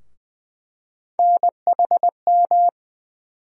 Morse code, heard as a keyed tone, about 20 wpm, 700 Hz. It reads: NHM
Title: USA Call Sign Postfixes - 3 Letters 20wpm